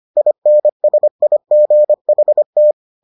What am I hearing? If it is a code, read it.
Morse code, heard as a keyed tone, about 25 wpm, 600 Hz. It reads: INSIGHT